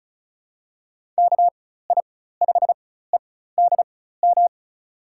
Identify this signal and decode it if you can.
Morse code, heard as a keyed tone, about 34 wpm, spaced as for 16 wpm, 700 Hz. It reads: KI5EDM
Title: USA Call Signs - 4x Character Spacing - 35wpm